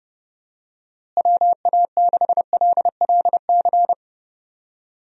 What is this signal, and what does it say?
Morse code, heard as a keyed tone, about 30 wpm, 700 Hz. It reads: WA6LLC
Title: USA Call Signs - 6 Characters - Repeated 3x - 30wpm